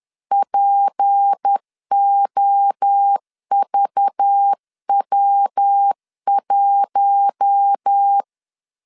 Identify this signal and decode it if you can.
Morse code, heard as a keyed tone, about 10 wpm, 800 Hz. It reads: POVW1